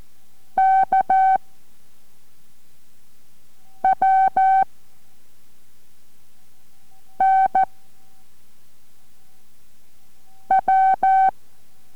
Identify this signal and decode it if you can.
Morse code, heard as a keyed tone, about 14 wpm, 770 Hz. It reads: K W N W